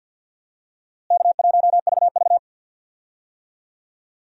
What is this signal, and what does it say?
Morse code, heard as a keyed tone, about 51 wpm, 700 Hz. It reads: K1VV
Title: USA Call Signs - 4 Characters - Repeated 3x - 50wpm